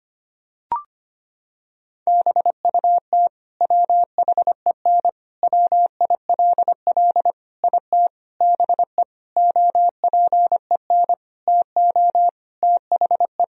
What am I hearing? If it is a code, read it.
Morse code, heard as a keyed tone, about 25 wpm, 700 Hz. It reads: BUT WHEN WILL IT BE OPEN TO THE
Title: Sentences from Top 1000 Words - Review Part 3 - 25wpm